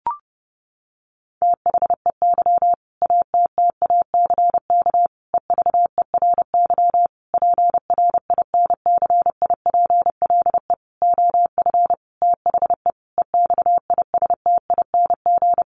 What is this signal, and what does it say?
Morse code, heard as a keyed tone, about 30 wpm, 700 Hz. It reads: THEY ATTACK EVERY PRINCIPLE OF THE EXISTING
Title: Sentences from Top 1500 Words 30wpm